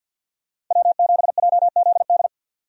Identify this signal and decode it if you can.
Morse code, heard as a keyed tone, about 52 wpm, 700 Hz. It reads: W7JZD
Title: USA Call Signs - 5 Characters - 50wpm